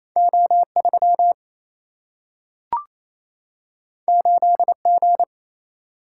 Morse code, heard as a keyed tone, about 28 wpm, 700 Hz. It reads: O 3 8 G
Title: Single-Letter Single-Number 28wpm